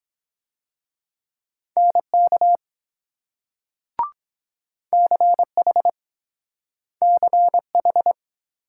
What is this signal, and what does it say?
Morse code, heard as a keyed tone, about 23 wpm, 700 Hz. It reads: NK CH CH